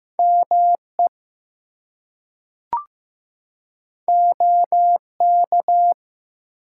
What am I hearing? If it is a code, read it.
Morse code, heard as a keyed tone, about 15 wpm, 700 Hz. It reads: ME OK
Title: Two Letter Words - 15wpm